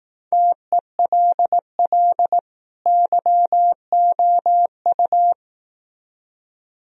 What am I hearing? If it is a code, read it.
Morse code, heard as a keyed tone, about 18 wpm, 700 Hz. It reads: TELL YOU
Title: Speed-Racing - Sets of 2 Words 15wpm